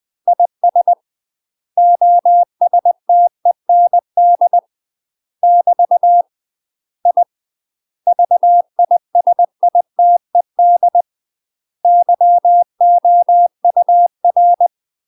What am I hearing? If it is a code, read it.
Morse code, heard as a keyed tone, about 20 wpm, 700 Hz. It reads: IS OSTEND <BT> I VISITED YOUR
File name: UK MASTER QSOs TRAINING / 20WPM / UK MASTER QSOs 0005